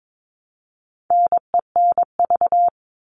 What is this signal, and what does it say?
Morse code, heard as a keyed tone, about 22 wpm, 700 Hz. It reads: NENV
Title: All Cut Numbers - 4-Digit 22wpm